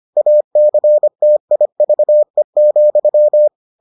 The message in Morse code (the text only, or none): ACTIVE,